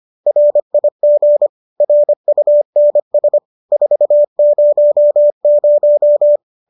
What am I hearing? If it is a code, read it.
Morse code, heard as a keyed tone, about 25 wpm, 590 Hz. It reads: RIG RUNS 400